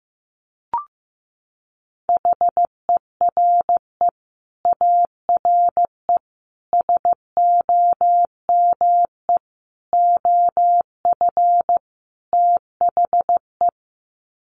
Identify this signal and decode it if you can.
Morse code, heard as a keyed tone, about 15 wpm, 700 Hz. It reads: HERE ARE SOME OF THE
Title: Sets of 5 Words Encore 15wpm